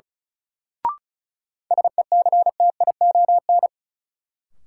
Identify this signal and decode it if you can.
Morse code, heard as a keyed tone, about 36 wpm, 700 Hz. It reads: SECTION
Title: Sentences from Top 1000 Words - Review Part 5 - New Word Drill - 35wpm